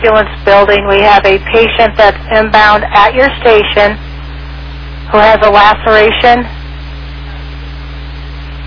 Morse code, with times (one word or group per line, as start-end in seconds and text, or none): none